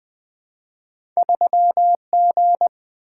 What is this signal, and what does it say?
Morse code, heard as a keyed tone, about 20 wpm, 700 Hz. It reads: 3G
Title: Two Letter-Number Combinations - 20wpm